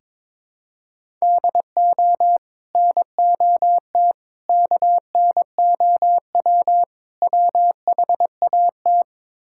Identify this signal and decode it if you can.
Morse code, heard as a keyed tone, about 22 wpm, 700 Hz. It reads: DO NOT KNOW WHAT
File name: practice / 22wpm / Build A Phrase - 5-Word Phrases - 22wpm